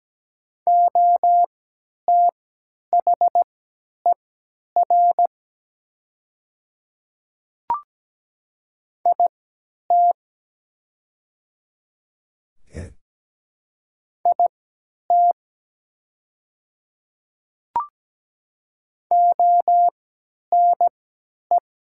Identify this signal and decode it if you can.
Morse code, heard as a keyed tone, about 17 wpm, spaced as for 10 wpm, 700 Hz. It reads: OTHER IT IT ONE
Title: 3x Character Spacing - Top 100 Words - 17wpm